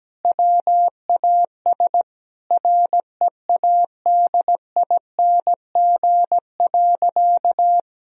Morse code, heard as a keyed tone, about 17 wpm, 700 Hz. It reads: WAS READING.
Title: Sentences from Top 1000 Words - Review Part 3 - 17wpm